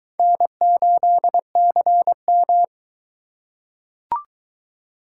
Morse code, heard as a keyed tone, about 22 wpm, 700 Hz. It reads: N8CM